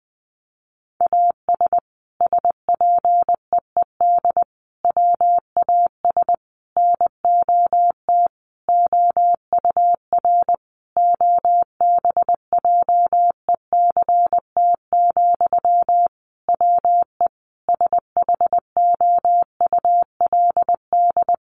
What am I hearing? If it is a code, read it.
Morse code, heard as a keyed tone, about 20 wpm, 700 Hz. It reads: AS SPEED WAS NOT OUR OBJECT, WE SHOULD
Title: Sentences from Top 800 Words 20wpm